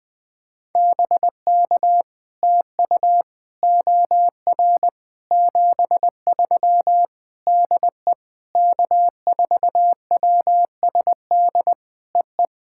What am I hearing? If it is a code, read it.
Morse code, heard as a keyed tone, about 21 wpm, 700 Hz. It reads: BK TU OR 73 DE K4WSD EE
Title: POTA QSO Practice - 20wpm